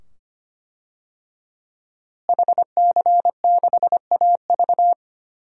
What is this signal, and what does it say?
Morse code, heard as a keyed tone, about 25 wpm, 700 Hz. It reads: HC6AV
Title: DXCC-100 Call Signs - 5 Characters - Repeated 2x - 25wpm